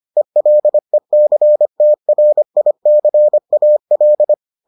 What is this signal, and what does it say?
Morse code, heard as a keyed tone, about 25 wpm, 600 Hz. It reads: ELECTRICAL